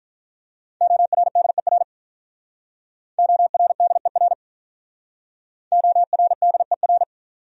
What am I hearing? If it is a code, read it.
Morse code, heard as a keyed tone, about 46 wpm, 700 Hz. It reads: ORDER ORDER ORDER